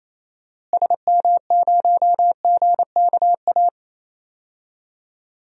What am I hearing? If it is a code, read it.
Morse code, heard as a keyed tone, about 28 wpm, 700 Hz. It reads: SM0GKA
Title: DXCC-100 Call Signs - 6 Characters - Repeated 3x - 28wpm